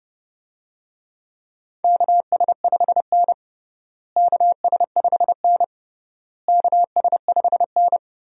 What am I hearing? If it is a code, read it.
Morse code, heard as a keyed tone, about 32 wpm, 700 Hz. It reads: KS5N KS5N KS5N
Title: USA Call Signs - 4 Characters - Repeated 3x - 30wpm